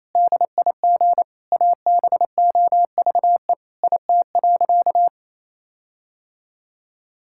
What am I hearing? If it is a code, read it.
Morse code, heard as a keyed tone, about 28 wpm, 700 Hz. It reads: DIG ABOVE IT.
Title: Sentences from Top 1600 Words 28wpm